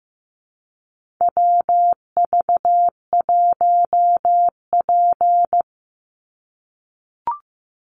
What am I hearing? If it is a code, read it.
Morse code, heard as a keyed tone, about 14 wpm, 700 Hz. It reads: WV1P
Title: USA Call Signs 15wpm